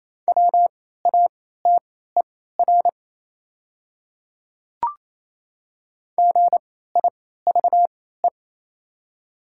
Morse code, heard as a keyed tone, about 28 wpm, spaced as for 16 wpm, 690 Hz. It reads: WATER GIVE
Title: 3x Character Spacing - Top 200 Words - 28wpm